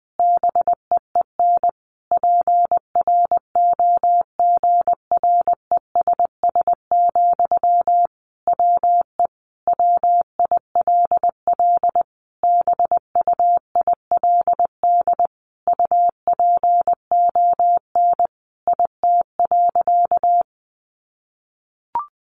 Morse code, heard as a keyed tone, about 20 wpm, 700 Hz. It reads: BEEN PROGRESS, WE WILL BUILD UPON IT.
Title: Sentences from Top 1400 Words 20wpm